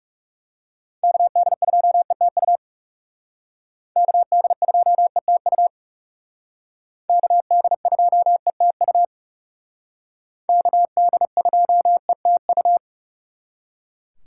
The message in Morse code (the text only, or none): KD2ETU KD2ETU KD2ETU KD2ETU